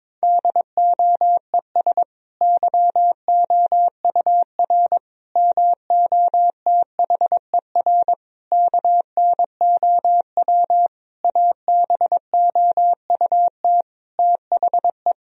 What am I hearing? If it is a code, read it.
Morse code, heard as a keyed tone, about 22 wpm, 700 Hz. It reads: DOES YOUR MOTHER KNOW ABOUT THE
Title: Sentences from Top 1000 Words - Review Part 4 - 22wpm